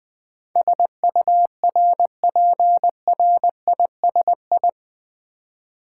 Morse code, heard as a keyed tone, about 20 wpm, 690 Hz. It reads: SURPRISI
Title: Build A Word - 4 Letters At A Time - 10-Letter Words - 20wpm